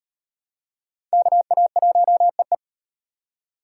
Morse code, heard as a keyed tone, about 38 wpm, 700 Hz. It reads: KA1EE